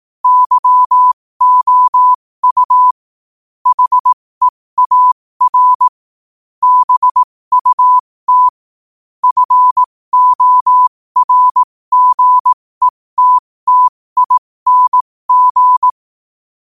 Morse code, heard as a keyed tone, about 18 wpm, spaced as for 15 wpm, 1000 Hz. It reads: YOU HEAR BUT FORGETTING
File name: qotdfiles / 15wpm